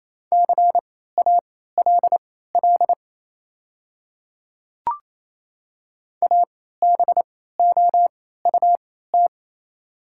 Morse code, heard as a keyed tone, about 28 wpm, spaced as for 16 wpm, 690 Hz. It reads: CALL ABOUT
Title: 3x Character Spacing - Top 500 Words - 28wpm